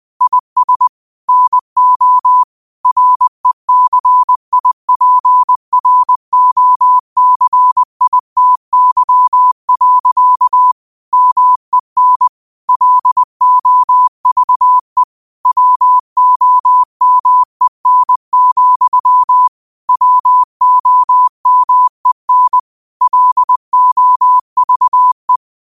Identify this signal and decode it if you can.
Morse code, heard as a keyed tone, about 20 wpm, 1000 Hz. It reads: IS NO RECIPROCITY. MEN LOVE WOMEN, WOMEN LOVE